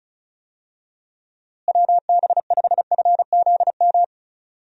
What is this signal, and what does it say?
Morse code, heard as a keyed tone, about 36 wpm, 700 Hz. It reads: WB5FZM
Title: USA Call Signs - 6 Characters - Repeated 3x - 35wpm